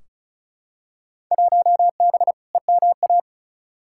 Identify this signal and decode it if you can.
Morse code, heard as a keyed tone, about 34 wpm, 700 Hz. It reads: 1B EMA
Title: Field Day Exchanges - 35wpm